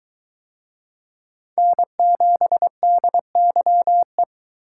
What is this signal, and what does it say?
Morse code, heard as a keyed tone, about 22 wpm, 700 Hz. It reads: N7DYE